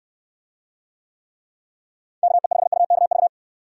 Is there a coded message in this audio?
Yes: Morse code from 2.2 to 3.3 s, heard as a keyed tone, about 69 wpm, 700 Hz. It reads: KE5UKV